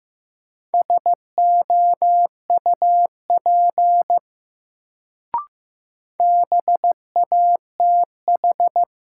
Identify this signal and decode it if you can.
Morse code, heard as a keyed tone, about 15 wpm, 700 Hz. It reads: SOUP BATH
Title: Sentences from Top 1600 Words - New Word Drill Alt - 15wpm